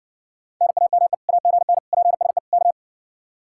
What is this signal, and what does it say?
Morse code, heard as a keyed tone, about 60 wpm, 700 Hz. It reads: NAME AGN PSE K